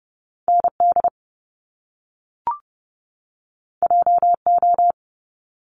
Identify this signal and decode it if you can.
Morse code, heard as a keyed tone, about 30 wpm, 700 Hz. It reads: ND JO
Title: Top Two Letter Combinations in Words 30wpm